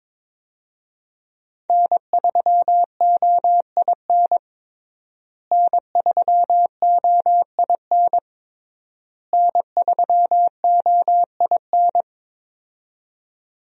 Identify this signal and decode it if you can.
Morse code, heard as a keyed tone, about 22 wpm, 700 Hz. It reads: N3OIN N3OIN N3OIN
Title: USA Call Signs - 5 Characters - Repeated 3x - 22wpm